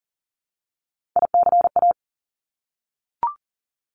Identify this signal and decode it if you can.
Morse code, heard as a keyed tone, about 41 wpm, 700 Hz. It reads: ICA
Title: Top Three Letter Combinations in Words 40wpm